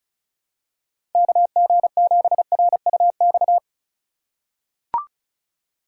Morse code, heard as a keyed tone, about 36 wpm, 700 Hz. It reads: KG7RUX
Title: USA Call Signs - 6 Characters - 4x Character Spacing - 35wpm